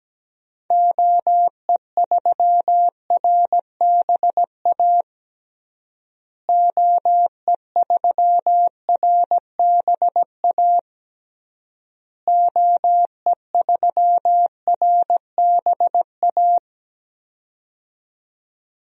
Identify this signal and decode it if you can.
Morse code, heard as a keyed tone, about 17 wpm, 700 Hz. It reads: OE3RBA OE3RBA OE3RBA
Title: DXCC-100 Call Signs - 6 Characters - Repeated 3x - 17wpm